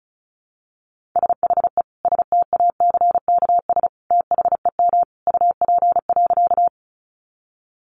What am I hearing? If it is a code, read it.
Morse code, heard as a keyed tone, about 34 wpm, 700 Hz. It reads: SHE STACKS THEM UP.
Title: Easy Phrases 35wpm